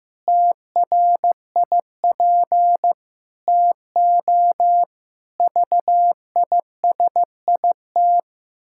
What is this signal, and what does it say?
Morse code, heard as a keyed tone, about 14 wpm, 700 Hz. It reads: TRIP TO VISIT
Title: Pangram Challenge v2 - 15wpm